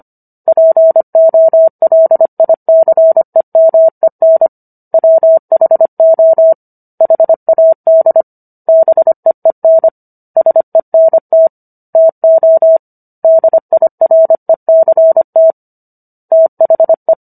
Morse code, heard as a keyed tone, about 25 wpm, 650 Hz. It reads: POLICEMEN WHO HAD BEEN SENT TO DIRECT THE